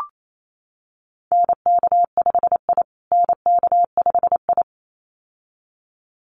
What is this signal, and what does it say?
Morse code, heard as a keyed tone, about 28 wpm, 700 Hz. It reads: NK5I NK5I
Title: USA Call Signs Repeated 2x 28wpm